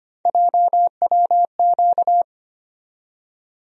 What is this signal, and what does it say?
Morse code, heard as a keyed tone, about 25 wpm, 700 Hz. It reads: JWQ
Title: USA Call Sign Postfixes - 3 Letters 25wpm